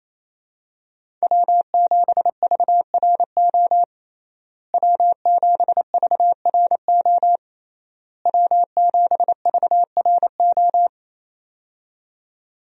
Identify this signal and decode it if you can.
Morse code, heard as a keyed tone, about 28 wpm, 700 Hz. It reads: W7VRO W7VRO W7VRO
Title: USA Call Signs - 5 Characters - Repeated 3x - 28wpm